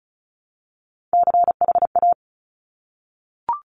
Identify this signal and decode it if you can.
Morse code, heard as a keyed tone, about 36 wpm, 700 Hz. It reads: CHA
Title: Top Three Letter Combinations in Words 35wpm